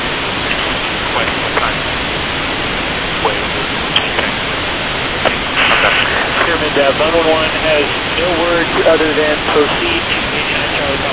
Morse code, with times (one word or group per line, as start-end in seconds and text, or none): none